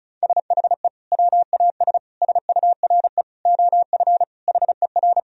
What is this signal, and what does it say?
Morse code, heard as a keyed tone, about 34 wpm, 700 Hz. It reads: SHE WAS SURE OF HER
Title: Sentences from Top 1000 Words - Review Part 1 - 35wpm